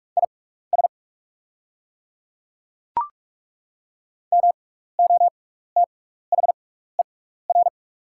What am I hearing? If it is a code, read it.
Morse code, heard as a keyed tone, about 43 wpm, spaced as for 15 wpm, 690 Hz. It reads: IS MOTHER
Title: 6x Character Spacing - Top 300 Words - 45wpm